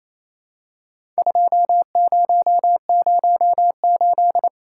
Morse code, heard as a keyed tone, about 28 wpm, 700 Hz. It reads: 2008
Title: Worked All Europe DX Contest - QTC Exchange - 28wpm